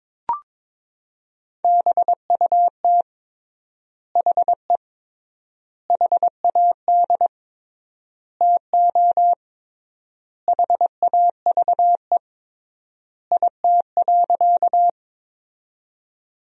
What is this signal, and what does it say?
Morse code, heard as a keyed tone, about 22 wpm, 700 Hz. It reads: BUT HE HAD TO HAVE IT.
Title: 3x Word-Spacing - Sentences from Top 100 Words - 22wpm